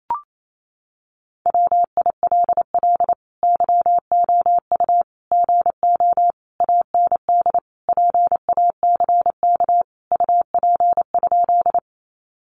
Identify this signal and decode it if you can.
Morse code, heard as a keyed tone, about 28 wpm, 700 Hz. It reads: WILL YOU GO AND PACK UP?
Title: Sentences from Top 1300 Words 28wpm